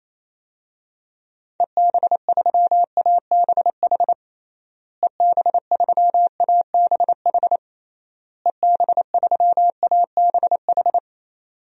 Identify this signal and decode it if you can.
Morse code, heard as a keyed tone, about 28 wpm, 700 Hz. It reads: EB3ABH EB3ABH EB3ABH
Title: DXCC-100 Call Signs - 6 Characters - Repeated 3x - 28wpm